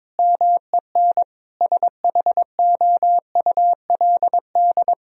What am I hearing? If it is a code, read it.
Morse code, heard as a keyed tone, about 22 wpm, 700 Hz. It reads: MEN SHOULD
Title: Sentences from Top 1000 Words - Review Part 1 - 22wpm